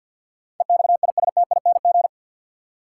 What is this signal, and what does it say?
Morse code, heard as a keyed tone, about 47 wpm, 700 Hz. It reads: EXISTING
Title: Sentences from Top 1500 Words - New Word Drill - 50wpm